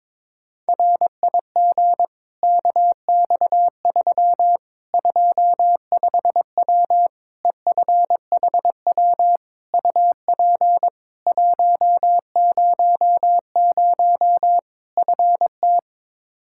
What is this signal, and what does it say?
Morse code, heard as a keyed tone, about 21 wpm, 700 Hz. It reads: RIG KX3 25W EFHW UP 100 FT
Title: Practice QSO - Exchange 2 - Info - 22wpm